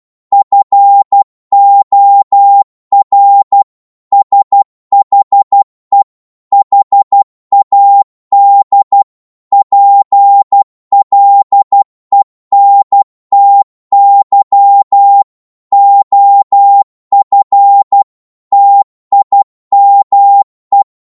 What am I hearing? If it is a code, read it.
Morse code, heard as a keyed tone, about 12 wpm, 800 Hz. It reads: FOR SHE HAD PLENTY OF TIME
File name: cwmp3 / psg-1-12wpm